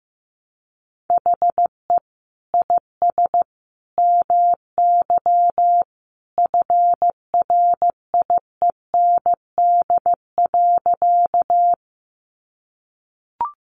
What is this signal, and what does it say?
Morse code, heard as a keyed tone, about 15 wpm, 700 Hz. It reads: HE IS MY FRIEND.
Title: Sentences from Top 300 Words 15wpm